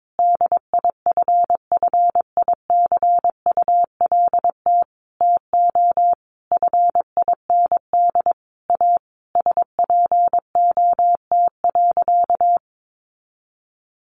Morse code, heard as a keyed tone, about 22 wpm, 700 Hz. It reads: DIFFICULT TO FIND A SPOT.
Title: Sentences from Top 1100 Words 22wpm